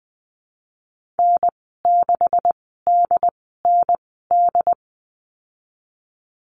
Morse code, heard as a keyed tone, about 20 wpm, spaced as for 15 wpm, 700 Hz. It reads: N6DND
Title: 161 Lesson 26g - Letter g - Call Signs 20wpm